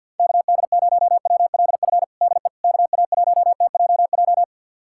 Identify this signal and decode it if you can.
Morse code, heard as a keyed tone, about 46 wpm, 700 Hz. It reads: KD0WLF DE KA1TJJ
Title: USA Callsign DE Callsign - 6 Characters - 50wpm